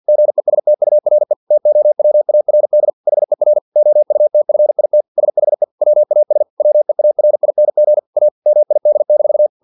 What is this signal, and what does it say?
Morse code, heard as a keyed tone, about 50 wpm, 600 Hz. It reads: GESTURE TOWARD HER OUTFIT SHE WAS WEARING A MID-
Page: Never 49wpm